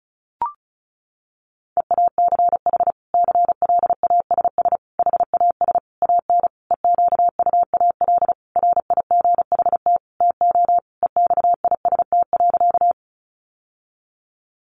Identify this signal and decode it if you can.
Morse code, heard as a keyed tone, about 34 wpm, 700 Hz. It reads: EACH CLASS HAS AN EQUAL RIGHT TO EXIST.
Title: Sentences from Top 1400 Words 35wpm